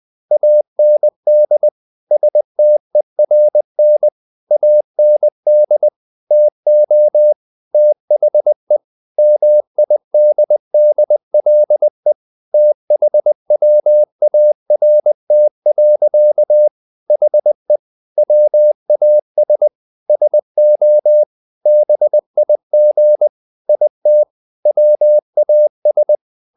Morse code, heard as a keyed tone, about 20 wpm, 600 Hz. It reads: AND STERN AND TO THE MIDDLE THWART. HE WAS SO BIG IT WAS
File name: the old man and the sea part 2 20wpm